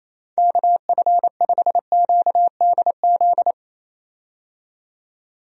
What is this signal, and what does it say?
Morse code, heard as a keyed tone, about 28 wpm, 700 Hz. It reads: KF5QDZ
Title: USA Call Signs - 6 Characters - 28wpm